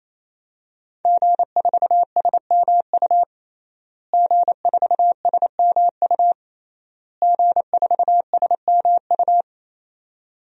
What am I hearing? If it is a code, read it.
Morse code, heard as a keyed tone, about 27 wpm, 700 Hz. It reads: G4SMU G4SMU G4SMU
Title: DXCC-100 Call Signs - 5 Characters - Repeated 3x - 28wpm